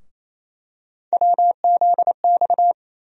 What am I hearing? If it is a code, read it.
Morse code, heard as a keyed tone, about 27 wpm, 700 Hz. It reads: WZX
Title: USA Call Sign Postfixes - 3 Letters 28wpm